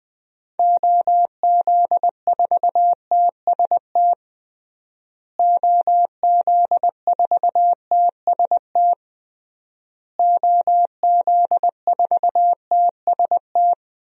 Morse code, heard as a keyed tone, about 20 wpm, 700 Hz. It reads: OZ4TST OZ4TST OZ4TST
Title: DXCC-100 Call Signs - 6 Characters - Repeated 3x - 20wpm